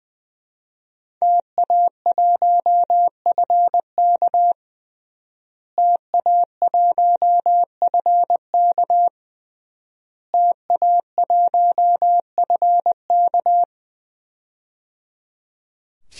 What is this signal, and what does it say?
Morse code, heard as a keyed tone, about 20 wpm, 700 Hz. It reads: TA1FK TA1FK TA1FK
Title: DXCC-100 Call Signs - 5 Characters - Repeated 3x - 20wpm